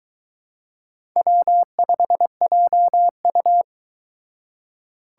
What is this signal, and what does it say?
Morse code, heard as a keyed tone, about 22 wpm, 700 Hz. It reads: W5JU